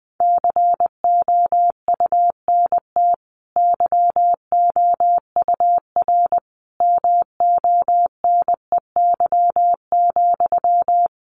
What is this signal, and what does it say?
Morse code, heard as a keyed tone, about 20 wpm, 700 Hz. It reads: COUNT YOUR MONEY,
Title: Sentences from Top 1500 Words 20wpm